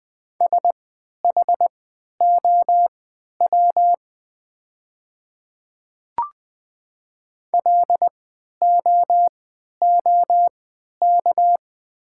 Morse code, heard as a keyed tone, about 20 wpm, spaced as for 11 wpm, 700 Hz. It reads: SHOW LOOK
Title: 3x Character Spacing - Top 300 Words - 20wpm